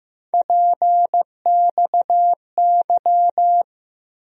Morse code, heard as a keyed tone, about 15 wpm, 700 Hz. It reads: PXY
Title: USA Call Sign Postfixes - 3 Letters 15wpm